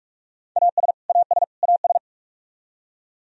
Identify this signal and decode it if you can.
Morse code, heard as a keyed tone, about 46 wpm, 700 Hz. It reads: AS AS AS
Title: Top 100 Words Repeated 45wpm